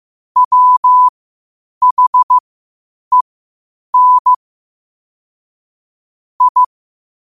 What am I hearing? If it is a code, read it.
Morse code, heard as a keyed tone, about 15 wpm, spaced as for 8 wpm, 1000 Hz. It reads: WHEN I
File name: qotdfiles / 8wpm